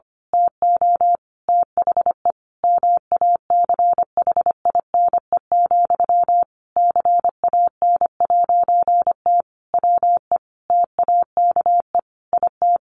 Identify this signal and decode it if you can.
Morse code, heard as a keyed tone, about 25 wpm, 700 Hz. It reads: TO THE MACHINE, CAN'T WE TAKE IT
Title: A Sound Of Thunder 25wpm